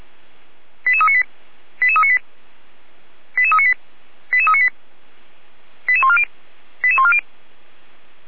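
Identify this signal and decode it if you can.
Five-tone selective call (ZVEI), heard as quick runs of five tones, tones of about 80 ms. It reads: 88398 88398 88398 88398 88140 88140
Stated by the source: ZVEI five-tone call received by SDR, from audio.f85360.feuerwehr-orlamuende.org fire brigade server.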